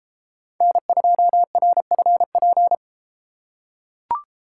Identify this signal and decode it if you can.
Morse code, heard as a keyed tone, about 32 wpm, 700 Hz. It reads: N2RFP